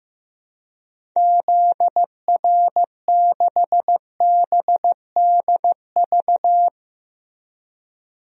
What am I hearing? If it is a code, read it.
Morse code, heard as a keyed tone, about 15 wpm, 700 Hz. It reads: ZR6BDV
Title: DXCC-100 Call Signs - 6 Characters - Repeated 2x - 15wpm